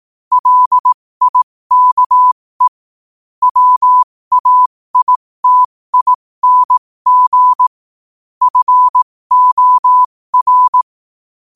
Morse code, heard as a keyed tone, about 18 wpm, spaced as for 15 wpm, 1000 Hz. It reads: LIKE WAITING FOR
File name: qotdfiles / 15wpm